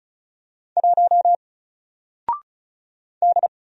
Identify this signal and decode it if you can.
Morse code, heard as a keyed tone, about 34 wpm, 700 Hz. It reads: 1 D
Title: Single Character (Letter:Number:Punctuation:Prosign) 35wpm